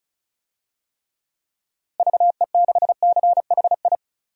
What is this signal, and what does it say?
Morse code, heard as a keyed tone, about 34 wpm, 700 Hz. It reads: VE6CHI
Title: DXCC-100 Call Signs - 6 Characters - Repeated 2x - 35wpm